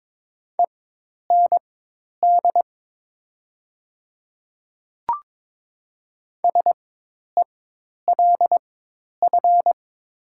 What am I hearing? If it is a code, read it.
Morse code, heard as a keyed tone, about 22 wpm, spaced as for 10 wpm, 700 Hz. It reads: END SELF
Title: 4x Character Spacing - Top 200 Words - 22wpm